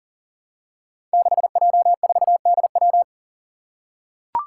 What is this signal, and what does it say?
Morse code, heard as a keyed tone, about 42 wpm, 700 Hz. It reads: 6J4DW
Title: Five Challenging Letter-Number Combinations 40wpm